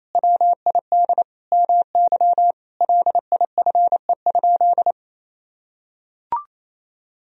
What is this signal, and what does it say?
Morse code, heard as a keyed tone, about 28 wpm, 700 Hz. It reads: WID MY LIFE?
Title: Sentences from Top 300 Words and Abbreviations 28wpm